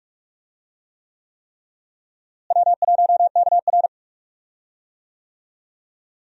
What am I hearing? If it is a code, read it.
Morse code, heard as a keyed tone, about 43 wpm, 700 Hz. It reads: W1KR
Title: USA Call Signs - 4 Characters - Repeated 3x - 45wpm